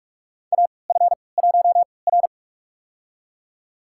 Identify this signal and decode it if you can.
Morse code, heard as a keyed tone, about 42 wpm, spaced as for 25 wpm, 690 Hz. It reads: AF1R
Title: USA Call Signs - 4 Characters - 3x Character Spacing - 45wpm